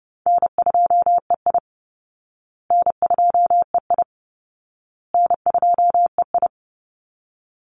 Morse code, heard as a keyed tone, about 30 wpm, 700 Hz. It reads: N2EI N2EI N2EI
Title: Call Signs Repeated 2x - 30wpm